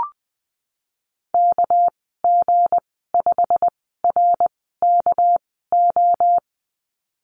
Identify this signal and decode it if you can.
Morse code, heard as a keyed tone, about 20 wpm, spaced as for 15 wpm, 700 Hz. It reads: KG5RKO
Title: 200 Lesson 32g - Letter k - Call Signs 20wpm